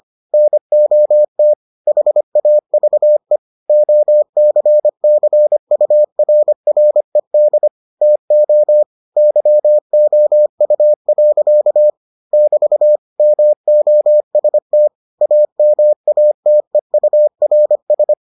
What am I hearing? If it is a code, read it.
Morse code, heard as a keyed tone, about 25 wpm, 600 Hz. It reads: NOT HAVE OCCURRED TO YOU. <BT> MOST AMATEURS